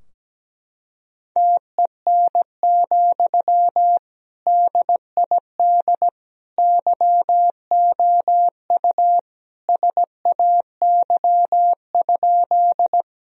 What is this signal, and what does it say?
Morse code, heard as a keyed tone, about 17 wpm, 700 Hz. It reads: TEN, DID YOU SAY?
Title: Sentences from Top 1000 Words - Review Part 2 - 17wpm